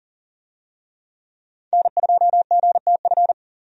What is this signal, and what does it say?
Morse code, heard as a keyed tone, about 40 wpm, 700 Hz. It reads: N2GTF
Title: USA Call Signs - 5 Characters - 40wpm